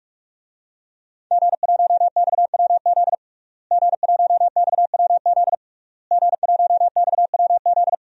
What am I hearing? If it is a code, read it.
Morse code, heard as a keyed tone, about 44 wpm, 700 Hz. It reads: G1XWZ G1XWZ G1XWZ
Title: DXCC-100 Call Signs - 5 Characters - Repeated 3x - 45wpm